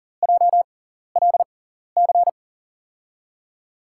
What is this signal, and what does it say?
Morse code, heard as a keyed tone, about 40 wpm, spaced as for 14 wpm, 690 Hz. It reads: JLC
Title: USA Call Signs - 5 Characters - 6x Character Spacing - 40wpm